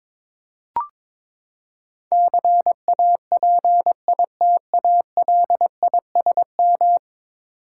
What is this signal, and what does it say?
Morse code, heard as a keyed tone, about 22 wpm, 700 Hz. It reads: CAPITALISM
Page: morse-code-ninja-practice-22wpm